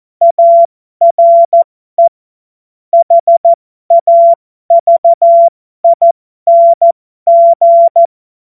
Morse code, heard as a keyed tone, about 14 wpm, spaced as for 12 wpm, 680 Hz. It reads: ARE HAVING
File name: Morsekurs / Witze